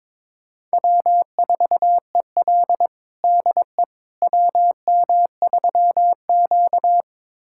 Morse code, heard as a keyed tone, about 22 wpm, 700 Hz. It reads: W4EL DE WM3Q
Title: USA Callsign DE Callsign - 4 Characters - 22wpm